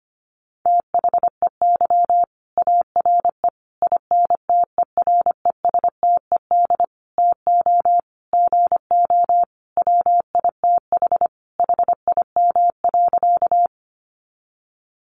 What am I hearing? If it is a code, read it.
Morse code, heard as a keyed tone, about 25 wpm, 700 Hz. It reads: THEY ARE INTERESTED TO GO WITH HIM.
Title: Sentences from Top 900 Words 25wpm